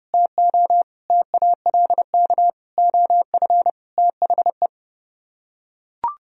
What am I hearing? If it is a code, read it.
Morse code, heard as a keyed tone, about 32 wpm, 700 Hz. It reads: TO TALK OF THE